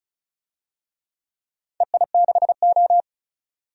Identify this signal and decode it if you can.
Morse code, heard as a keyed tone, about 36 wpm, 700 Hz. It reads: EI6O
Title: DXCC-100 Call Signs - 4 Characters - 35wpm